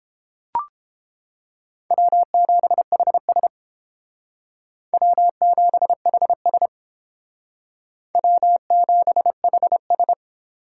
Morse code, heard as a keyed tone, about 31 wpm, 700 Hz. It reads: W7HS W7HS W7HS